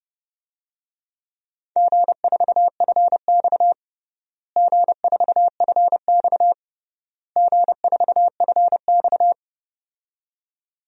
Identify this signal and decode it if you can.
Morse code, heard as a keyed tone, about 30 wpm, 700 Hz. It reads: G4FX G4FX G4FX
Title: DXCC-100 Call Signs - 4 Characters - Repeated 3x - 30wpm